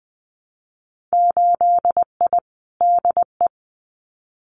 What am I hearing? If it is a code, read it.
Morse code, heard as a keyed tone, about 20 wpm, 700 Hz. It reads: 8I DE
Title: Winter Field Day Exchanges Encore 20wpm